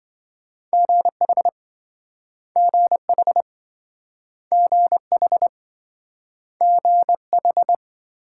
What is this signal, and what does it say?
Morse code, heard as a keyed tone, about 27 wpm, 700 Hz. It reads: GH GH GH GH